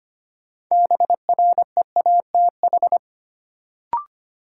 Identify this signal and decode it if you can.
Morse code, heard as a keyed tone, about 25 wpm, 700 Hz. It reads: BREATH
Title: Sets of 5 Words Encore - New Word Drill - 25wpm